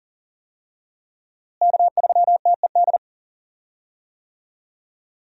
Morse code, heard as a keyed tone, about 37 wpm, 700 Hz. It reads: K3TED
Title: USA Call Signs - 5 Characters - Repeated 2x - 40wpm